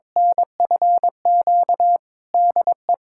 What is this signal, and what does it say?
Morse code, heard as a keyed tone, about 21 wpm, 700 Hz. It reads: NFQ DE
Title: USA Callsign DE Callsign - 6 Characters - 22wpm